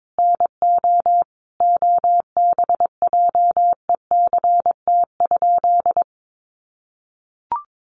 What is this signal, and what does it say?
Morse code, heard as a keyed tone, about 22 wpm, 700 Hz. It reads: NO OBJECT?
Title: Sentences from Top 500 Words 22wpm